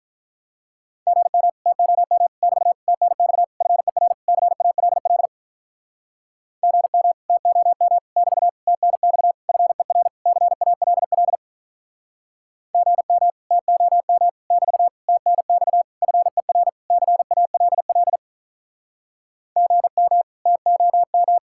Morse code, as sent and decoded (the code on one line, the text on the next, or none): --. -- / - --- -- / -...- / - -. -..- / ..-. .-. / -.-. .- .-.. .-.. / --. -- / - --- -- / -...- / - -. -..- / ..-. .-. / -.-. .- .-.. .-.. / --. -- / - --- -- / -...- / - -. -..- / ..-. .-. / -.-. .- .-.. .-.. / --. -- / - --- --
GM TOM <BT> TNX FER CALL GM TOM <BT> TNX FER CALL GM TOM <BT> TNX FER CALL GM TOM